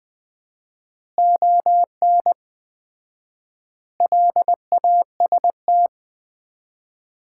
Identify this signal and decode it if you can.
Morse code, heard as a keyed tone, about 20 wpm, 700 Hz. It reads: ON LAST
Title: Common QSO Phrases - 4x Word-Spacing - 20wpm